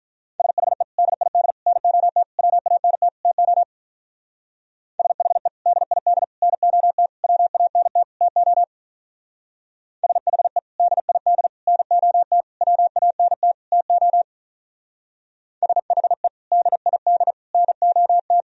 Morse code, heard as a keyed tone, about 46 wpm, 700 Hz. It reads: SHE DID NOT WANT TO SHE DID NOT WANT TO SHE DID NOT WANT TO SHE DID NOT